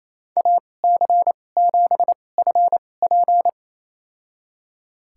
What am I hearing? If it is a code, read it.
Morse code, heard as a keyed tone, about 28 wpm, spaced as for 20 wpm, 700 Hz. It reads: AC7FP
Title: USA Call Signs - 2x Character Spacing - 28wpm